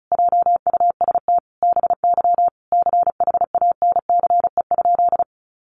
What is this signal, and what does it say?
Morse code, heard as a keyed tone, about 34 wpm, 700 Hz. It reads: JUST BY CHANCE?
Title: Sentences from Top 800 Words 35wpm